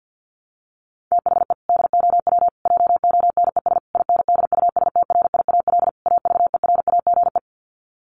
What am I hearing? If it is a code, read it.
Morse code, heard as a keyed tone, about 48 wpm, 700 Hz. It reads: THE DOW JONES INDUSTRIAL AVERAGE
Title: Sets of 5 Words 50wpm